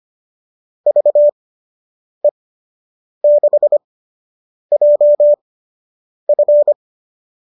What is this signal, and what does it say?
Morse code, heard as a keyed tone, about 25 wpm, spaced as for 8 wpm, 590 Hz. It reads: VE6JF